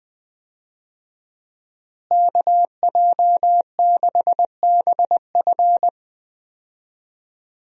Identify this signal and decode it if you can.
Morse code, heard as a keyed tone, about 20 wpm, 700 Hz. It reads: KJ6BF
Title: USA Call Signs - 5 Characters - Repeated 2x - 20wpm